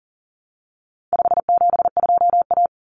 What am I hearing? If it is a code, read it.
Morse code, heard as a keyed tone, about 40 wpm, 700 Hz. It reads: 572A
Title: Common Cut Numbers - 4-Digit 40wpm